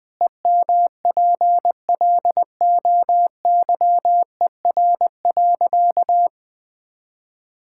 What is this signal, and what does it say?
Morse code, heard as a keyed tone, about 20 wpm, 700 Hz. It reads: EMPLOYER.
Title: Pangram Challenge v2 - 20wpm